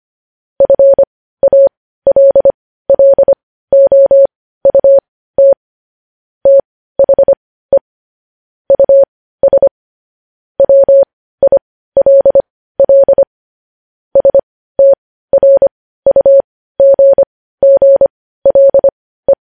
Morse code, heard as a keyed tone, about 25 wpm, spaced as for 15 wpm, 550 Hz. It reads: FALLOUT THE US WILL STRUGGLE
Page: News Headlines in Morse Code at 15 WPM